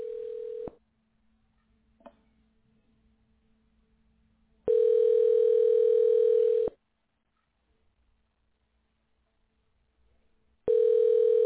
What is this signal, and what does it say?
Ringback tone (North America), 440 + 480 Hz.